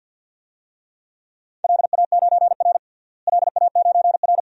Morse code, heard as a keyed tone, about 53 wpm, 700 Hz. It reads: LA9R LA9R
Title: DXCC-100 Call Signs - 4 Characters - Repeated 2x - 50wpm